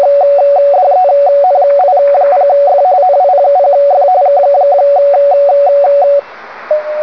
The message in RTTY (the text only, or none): DENI0F NI0F KKK